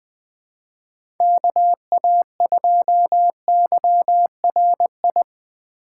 Morse code, heard as a keyed tone, about 20 wpm, 700 Hz. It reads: KA2YRI
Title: USA Call Signs - 6 Characters - 3x Character Spacing - 20wpm